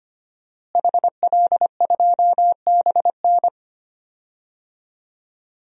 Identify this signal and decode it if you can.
Morse code, heard as a keyed tone, about 25 wpm, 700 Hz. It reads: HL2BN
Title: DXCC-100 Call Signs - 5 Characters - Repeated 2x - 25wpm